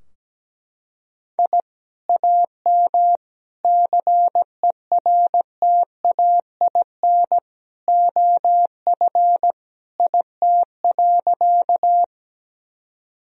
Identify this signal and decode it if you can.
Morse code, heard as a keyed tone, about 17 wpm, 700 Hz. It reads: I AM CERTAIN OF IT.
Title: Sentences from Top 1000 Words - Review Part 2 - 17wpm